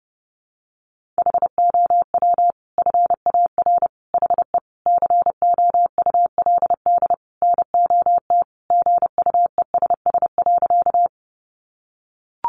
Morse code, heard as a keyed tone, about 30 wpm, 700 Hz. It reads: HOW FAR HE COULD NOT GUESS.
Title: Sentences from Top 1500 Words 30wpm